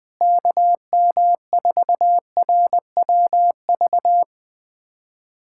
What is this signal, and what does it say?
Morse code, heard as a keyed tone, about 20 wpm, 700 Hz. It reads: KM4RWV
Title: USA Callsign DE Callsign 20wpm